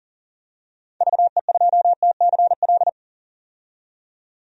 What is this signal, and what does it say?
Morse code, heard as a keyed tone, about 39 wpm, 700 Hz. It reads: VE2TCL
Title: DXCC-100 Call Signs - 6 Characters - 40wpm